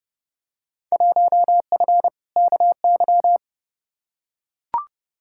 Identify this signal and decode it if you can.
Morse code, heard as a keyed tone, about 30 wpm, 700 Hz. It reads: 1F KY
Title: Field Day Exchanges Encore - 30wpm